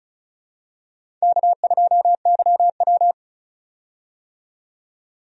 Morse code, heard as a keyed tone, about 35 wpm, 700 Hz. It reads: K2YW
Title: USA Call Signs - 4 Characters - 35wpm